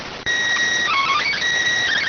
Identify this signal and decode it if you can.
SSTV (slow-scan television), PD 290.